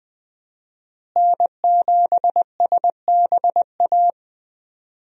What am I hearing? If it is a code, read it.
Morse code, heard as a keyed tone, about 20 wpm, 700 Hz. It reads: N7SBA
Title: USA Call Signs - 5 Characters - Repeated 3x - 20wpm